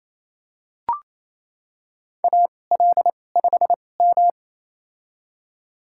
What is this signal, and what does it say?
Morse code, heard as a keyed tone, about 28 wpm, spaced as for 20 wpm, 700 Hz. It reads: AL5M
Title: USA Call Signs - 2x Character Spacing - 28wpm